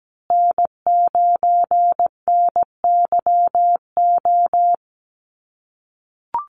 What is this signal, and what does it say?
Morse code, heard as a keyed tone, about 17 wpm, 700 Hz. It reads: N9NYO
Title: USA Call Signs Repeated 2x 17wpm